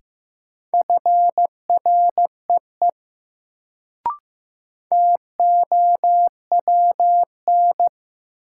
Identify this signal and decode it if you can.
Morse code, heard as a keyed tone, about 15 wpm, 700 Hz. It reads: FREE TOWN
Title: Sentences from Top 500 Words - New Word Drill - 15wpm